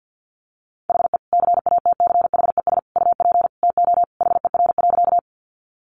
Morse code, heard as a keyed tone, about 50 wpm, 700 Hz. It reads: HE CATCHES UP TO HER.
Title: Additional Easy Phrases 50wpm